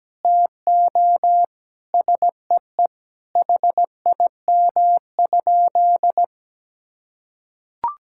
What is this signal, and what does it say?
Morse code, heard as a keyed tone, about 17 wpm, 700 Hz. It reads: TO SEE HIM?
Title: Sentences from Top 1600 Words 17wpm